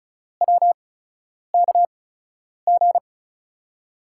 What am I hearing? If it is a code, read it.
Morse code, heard as a keyed tone, about 36 wpm, spaced as for 10 wpm, 700 Hz. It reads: WKG
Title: USA Call Signs - 5 Characters - 8x Character Spacing - 35wpm